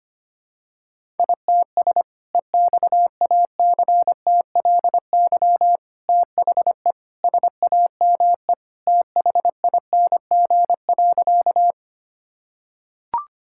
Morse code, heard as a keyed tone, about 25 wpm, 700 Hz. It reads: ITS EXACTLY THE SAME THING.
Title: Sentences from Top 1000 Words - Review Part 6 - 25wpm